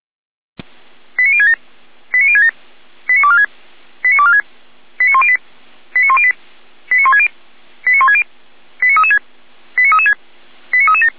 Five-tone selective call (ZVEI), heard as quick runs of five tones, tones of about 80 ms. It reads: 89067 89067 89246 89246 89108 89108 89160 89160 89336 89336 89337